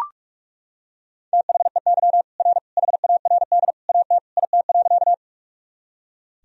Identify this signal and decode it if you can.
Morse code, heard as a keyed tone, about 41 wpm, 700 Hz. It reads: THEY R HARD AT IT.
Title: Sentences from Top 300 Words and Abbreviations 45wpm